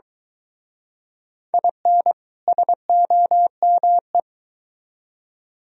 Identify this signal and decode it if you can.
Morse code, heard as a keyed tone, about 23 wpm, 700 Hz. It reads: IN SOME